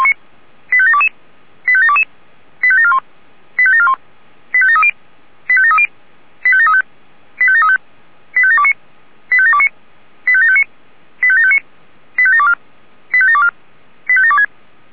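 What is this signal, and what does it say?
Five-tone selective call (ZVEI), heard as quick runs of five tones, tones of about 70 ms. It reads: 85722 85722 85741 85741 85730 85730 85735 85735 85729 85729 85759 85759 85724 85724 85726
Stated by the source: ZVEI five-tone selective call received via SDR.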